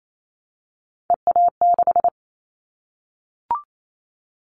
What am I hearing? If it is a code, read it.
Morse code, heard as a keyed tone, about 28 wpm, 700 Hz. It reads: EA6
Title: Callsign Prefixes for Top 50 DXCC - 28wpm